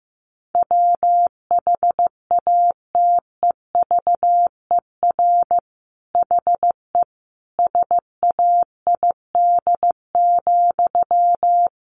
Morse code, heard as a keyed tone, about 15 wpm, 700 Hz. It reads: WHATEVER HE SAID,
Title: Sentences from Top 1100 Words 15wpm